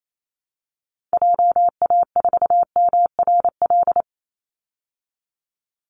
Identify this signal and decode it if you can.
Morse code, heard as a keyed tone, about 28 wpm, 700 Hz. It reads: JA4MRL
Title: Call Signs Repeated 3x - 28wpm